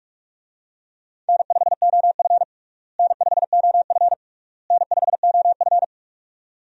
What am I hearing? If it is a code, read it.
Morse code, heard as a keyed tone, about 42 wpm, 700 Hz. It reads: N5OF N5OF N5OF
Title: USA Call Signs - 4 Characters - Repeated 3x - 45wpm